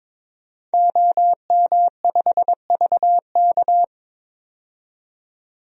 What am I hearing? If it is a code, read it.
Morse code, heard as a keyed tone, about 22 wpm, 700 Hz. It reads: OM5VK
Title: DXCC-100 Call Signs - 5 Characters - 22wpm